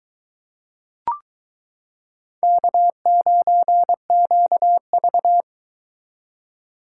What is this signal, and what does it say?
Morse code, heard as a keyed tone, about 22 wpm, 700 Hz. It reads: K9QV